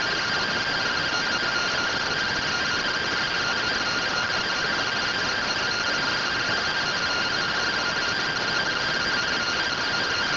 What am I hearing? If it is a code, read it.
Two-tone teleprinter signal, about 170 Hz shift.